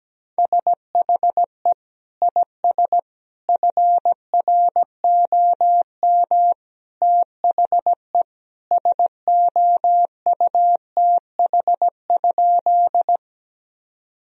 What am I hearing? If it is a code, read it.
Morse code, heard as a keyed tone, about 17 wpm, 700 Hz. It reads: SHE IS FROM THE SOUTH?
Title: Sentences from Top 1000 Words - Review Part 2 - 17wpm